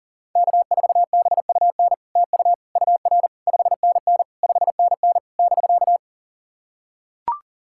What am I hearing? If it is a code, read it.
Morse code, heard as a keyed tone, about 40 wpm, 700 Hz. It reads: K4BUN TU UR 5NN 5NN <BK>
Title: SOTA QSO Practice 40wpm